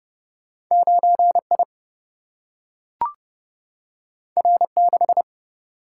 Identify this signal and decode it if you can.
Morse code, heard as a keyed tone, about 30 wpm, 700 Hz. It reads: 9I R6
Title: Single-Letter Single-Number 30wpm